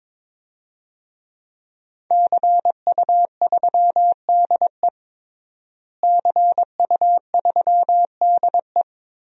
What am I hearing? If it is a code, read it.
Morse code, heard as a keyed tone, about 22 wpm, 700 Hz. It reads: CU3DE CU3DE
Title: DXCC-100 Call Signs - 5 Characters - Repeated 3x - 22wpm